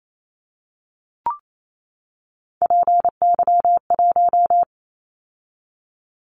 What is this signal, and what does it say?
Morse code, heard as a keyed tone, about 28 wpm, 700 Hz. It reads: PY1
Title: Callsign Prefixes for Top 100 DXCC - 28wpm